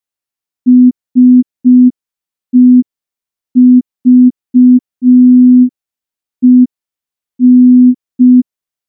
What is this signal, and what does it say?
Morse code, heard as a keyed tone, about 5 wpm, 250 Hz. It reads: SEVEN
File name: morse / morse-training-05WPM